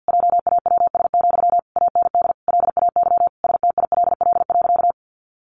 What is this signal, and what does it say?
Morse code, heard as a keyed tone, about 50 wpm, 700 Hz. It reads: JAWS, AND LAY STILL.
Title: A Sound Of Thunder 50wpm